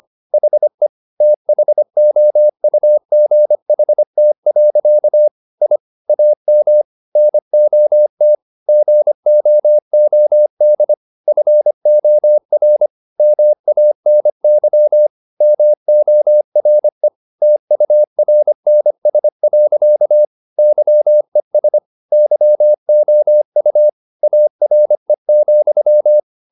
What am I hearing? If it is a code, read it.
Morse code, heard as a keyed tone, about 25 wpm, 600 Hz. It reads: HE THOUGHT. I AM NOT GOOD FOR MANY MORE TURNS. YES YOU ARE,